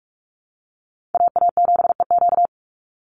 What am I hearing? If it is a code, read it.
Morse code, heard as a keyed tone, about 44 wpm, 700 Hz. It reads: AA7EQ
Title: NA SOTA Chaser Callsigns 45wpm